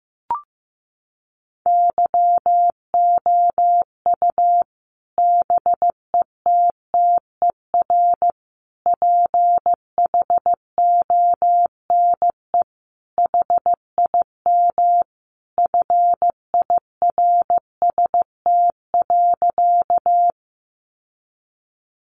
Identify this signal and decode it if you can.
Morse code, heard as a keyed tone, about 15 wpm, 700 Hz. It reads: YOU BETTER PHONE HIM FIRST.
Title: Sentences from Top 1100 Words 15wpm